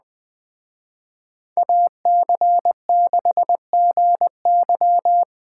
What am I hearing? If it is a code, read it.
Morse code, heard as a keyed tone, about 20 wpm, 700 Hz. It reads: AC6GY